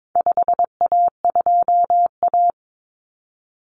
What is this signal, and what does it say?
Morse code, heard as a keyed tone, about 22 wpm, 700 Hz. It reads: 5A2A
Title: Common Cut Numbers - 4-Digit 22wpm